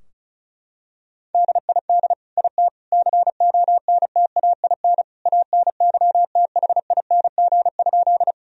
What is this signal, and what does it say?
Morse code, heard as a keyed tone, about 36 wpm, 700 Hz. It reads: DID IT CONTAIN ANYTHING?
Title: Sentences from Top 1000 Words - Review Part 2 - 35wpm